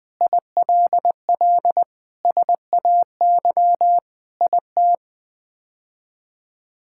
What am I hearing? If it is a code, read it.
Morse code, heard as a keyed tone, about 21 wpm, 700 Hz. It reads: ILL SAY IT
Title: Sets of 8 Words - 20wpm